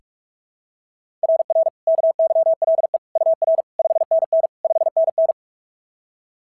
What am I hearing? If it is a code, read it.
Morse code, heard as a keyed tone, about 44 wpm, 690 Hz. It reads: RR KYLE UR 5NN 5NN
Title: SOTA QSO Practice 45wpm